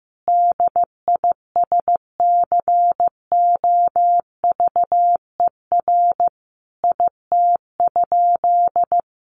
Morse code, heard as a keyed tone, about 15 wpm, 700 Hz. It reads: DISCOVER IT?
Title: Sentences from Top 1100 Words 15wpm